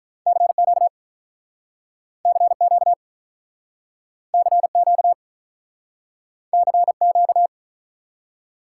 Morse code, heard as a keyed tone, about 47 wpm, 700 Hz. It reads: CQ CQ CQ CQ